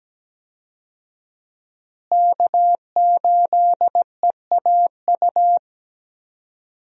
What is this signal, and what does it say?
Morse code, heard as a keyed tone, about 17 wpm, 700 Hz. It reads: K8EAU
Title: USA Call Signs - 5 Characters - Repeated 2x - 17wpm